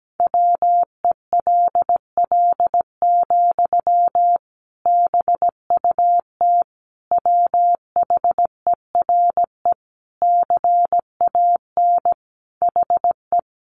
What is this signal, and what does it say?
Morse code, heard as a keyed tone, about 17 wpm, 700 Hz. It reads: WELL, BUT WHERE CAN HE
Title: Sentences from Top 500 Words 17wpm